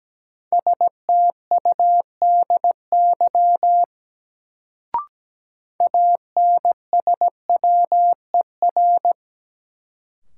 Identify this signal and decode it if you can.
Morse code, heard as a keyed tone, about 17 wpm, 700 Hz. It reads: STUDY ANSWER
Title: Sentences from Top 300 Words - New Word Drill - 17wpm